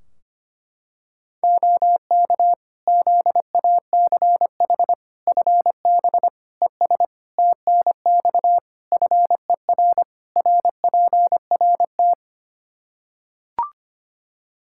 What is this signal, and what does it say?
Morse code, heard as a keyed tone, about 25 wpm, 700 Hz. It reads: OK ZACH FB ES TNX FER RPRT
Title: Practice QSO - Exchange 2 - Info - 25wpm